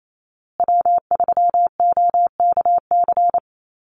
Morse code, heard as a keyed tone, about 28 wpm, 700 Hz. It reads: W3OKC
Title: Call Signs Repeated 2x - 28wpm